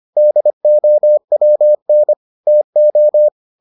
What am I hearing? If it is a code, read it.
Morse code, heard as a keyed tone, about 25 wpm, 600 Hz. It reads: DOWN TO